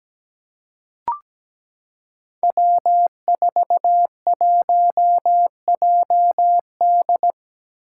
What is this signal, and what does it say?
Morse code, heard as a keyed tone, about 17 wpm, 700 Hz. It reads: W41JD
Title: Five Challenging Letter-Number Combinations 17wpm